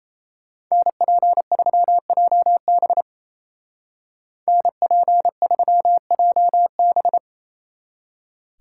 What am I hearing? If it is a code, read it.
Morse code, heard as a keyed tone, about 34 wpm, 700 Hz. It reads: NP3JB NP3JB
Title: Speed Racing - USA Call Signs 28wpm